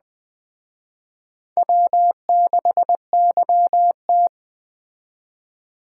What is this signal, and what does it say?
Morse code, heard as a keyed tone, about 20 wpm, 700 Hz. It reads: W6YT